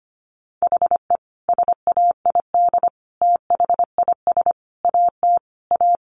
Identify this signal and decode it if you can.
Morse code, heard as a keyed tone, about 25 wpm, 700 Hz. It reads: HE SAID THIS AT A
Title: Sentences from Top 1000 Words 25wpm